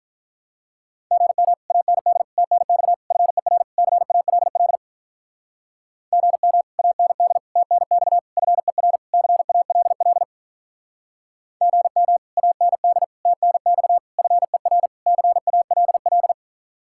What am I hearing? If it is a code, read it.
Morse code, heard as a keyed tone, about 44 wpm, 700 Hz. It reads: GM AND TNX FER CALL GM AND TNX FER CALL GM AND TNX FER CALL